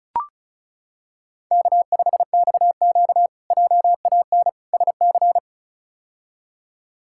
Morse code, heard as a keyed tone, about 36 wpm, 700 Hz. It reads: K5XQ JAN SC
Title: K1USN SST QSO Practice - 35wpm